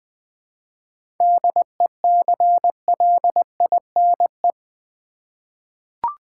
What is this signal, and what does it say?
Morse code, heard as a keyed tone, about 20 wpm, 700 Hz. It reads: DECLINE
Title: 7 Letter Words - First 3 Letters Given - 20wpm